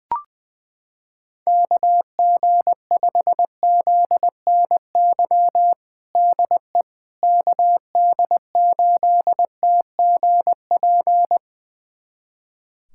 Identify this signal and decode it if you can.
Morse code, heard as a keyed tone, about 20 wpm, 700 Hz. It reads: KG5ZNY DE KD8TGP
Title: USA Callsign DE Callsign - 6 Characters - 20wpm